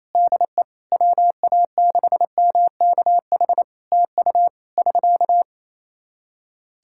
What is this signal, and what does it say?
Morse code, heard as a keyed tone, about 27 wpm, 700 Hz. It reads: DE WA6MKH TU <SK>
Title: K1USN SST QSO Practice 28wpm